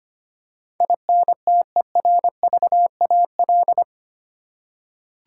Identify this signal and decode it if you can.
Morse code, heard as a keyed tone, about 25 wpm, 700 Hz. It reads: INTERVAL
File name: Eight Letter Words - 25wpm